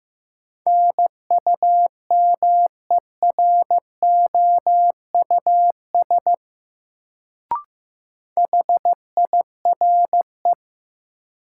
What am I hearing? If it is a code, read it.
Morse code, heard as a keyed tone, about 15 wpm, 700 Hz. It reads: NUMEROUS HIRE
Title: Sentences from Top 1400 Words - New Word Drill Alt - 15wpm